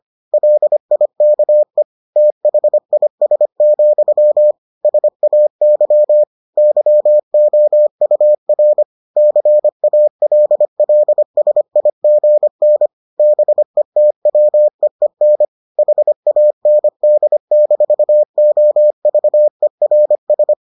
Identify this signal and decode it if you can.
Morse code, heard as a keyed tone, about 25 wpm, 600 Hz. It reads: LIKE THIS, SAY YOUR CALLSIGN BETWEEN HAND-OVERS